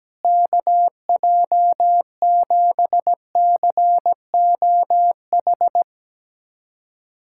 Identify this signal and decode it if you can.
Morse code, heard as a keyed tone, about 17 wpm, 700 Hz. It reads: KJ7COH
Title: USA Callsign DE Callsign 17wpm